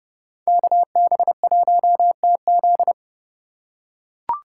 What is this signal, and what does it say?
Morse code, heard as a keyed tone, about 30 wpm, 700 Hz. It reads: KB1TZ